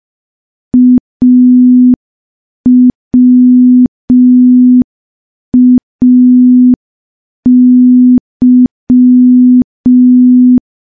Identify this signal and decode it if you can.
Morse code, heard as a keyed tone, about 5 wpm, 250 Hz. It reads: AWAY
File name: morse / morse-training-05WPM